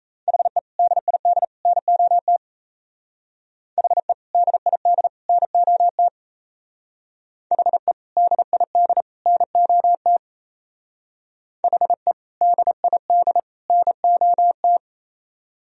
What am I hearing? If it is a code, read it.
Morse code, heard as a keyed tone, about 34 wpm, 700 Hz. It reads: HE DID NOT HE DID NOT HE DID NOT HE DID NOT
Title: Speed-Racing - Sets of 3 Words 28wpm